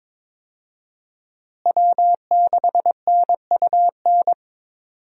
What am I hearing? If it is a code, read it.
Morse code, heard as a keyed tone, about 22 wpm, 700 Hz. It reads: W6NUN
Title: USA Call Signs - 5 Characters - Repeated 2x - 22wpm